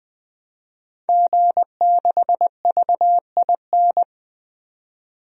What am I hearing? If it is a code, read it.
Morse code, heard as a keyed tone, about 20 wpm, 700 Hz. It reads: G6VIN
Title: DXCC-100 Call Signs - 5 Characters - 20wpm